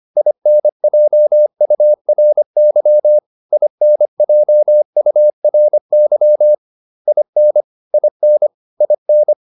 Morse code, heard as a keyed tone, about 25 wpm, 600 Hz. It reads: INJURY INJURY IN IN IN